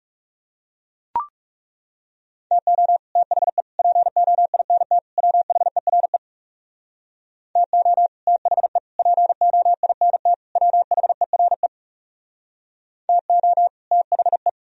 Morse code, heard as a keyed tone, about 37 wpm, 700 Hz. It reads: TO THE POINT WHERE TO THE POINT WHERE TO THE
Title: Speed-Racing - Sets of 4 Words 30wpm